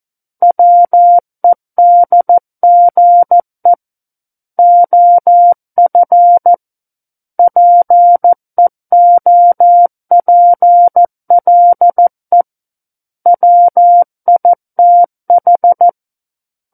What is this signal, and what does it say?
Morse code, heard as a keyed tone, about 14 wpm, 700 Hz. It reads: WEDGE OF PEOPLE WITH
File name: Book One / 05-Chapter 5 - 14 WPM